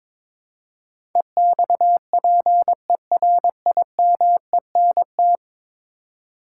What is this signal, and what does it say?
Morse code, heard as a keyed tone, about 22 wpm, 700 Hz. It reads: EXPERIMENT
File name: practice / 22wpm / 10 Letter Words - First 3 Letters Given - 22wpm